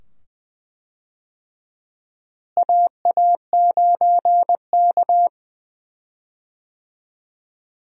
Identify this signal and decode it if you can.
Morse code, heard as a keyed tone, about 21 wpm, 700 Hz. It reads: AA9K
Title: USA Call Signs - 4 Characters - Repeated 2x - 20wpm